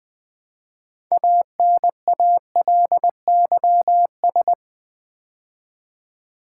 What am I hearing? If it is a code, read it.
Morse code, heard as a keyed tone, about 20 wpm, 700 Hz. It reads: ANALYS
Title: Build A Word - 2 Letters At A Time - 8-Letter Words - 20wpm